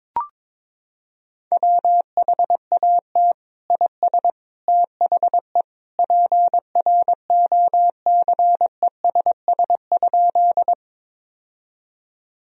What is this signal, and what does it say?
Morse code, heard as a keyed tone, about 22 wpm, 700 Hz. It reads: WHAT IS THE PROCESS?
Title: Sentences from Top 1000 Words - Review Part 3 - 22wpm